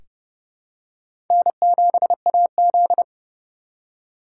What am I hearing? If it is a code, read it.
Morse code, heard as a keyed tone, about 30 wpm, 700 Hz. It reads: N7AZ
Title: K1USN SST QSO Practice - 30wpm